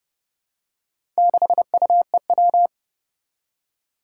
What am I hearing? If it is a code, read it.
Morse code, heard as a keyed tone, about 28 wpm, 700 Hz. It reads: 6UEW
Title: Four Challenging Letter-Number Combinations 30wpm